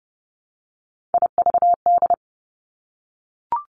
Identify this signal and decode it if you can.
Morse code, heard as a keyed tone, about 30 wpm, 700 Hz. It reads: IVD
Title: Three Challenging Letter-Number Combinations 30wpm